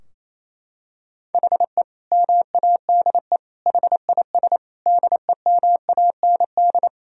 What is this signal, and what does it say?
Morse code, heard as a keyed tone, about 28 wpm, 700 Hz. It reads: HE MADE HIS DEMAND
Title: Sentences from Top 1000 Words - Review Part 4 - 28wpm